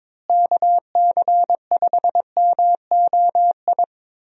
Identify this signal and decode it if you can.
Morse code, heard as a keyed tone, about 22 wpm, 700 Hz. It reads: KC5MOI
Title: USA Call Signs - 6 Characters - Repeated 3x - 22wpm